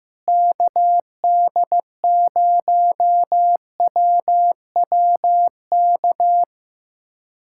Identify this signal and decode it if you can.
Morse code, heard as a keyed tone, about 15 wpm, 700 Hz. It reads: KD0WWK
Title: USA Callsign DE Callsign 15wpm